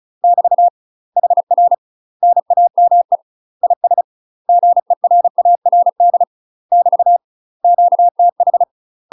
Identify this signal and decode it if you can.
Morse code, heard as a keyed tone, about 36 wpm, 700 Hz. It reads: <BT> HR NAME IS GERARD <BT> QTH
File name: UK MASTER QSOs TRAINING / 35WPM / UK MASTER QSOs 0007